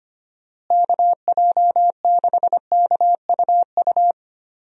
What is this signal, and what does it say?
Morse code, heard as a keyed tone, about 25 wpm, 700 Hz. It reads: KJ6KUU
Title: USA Call Signs - 6 Characters - Repeated 3x - 25wpm